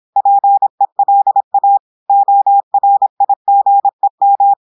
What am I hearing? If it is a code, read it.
Morse code, heard as a keyed tone, about 26 wpm, 800 Hz. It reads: PELA ORIGEM
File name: audios / portugais / portugais texte en clair 1 vitesse 26 mots minutes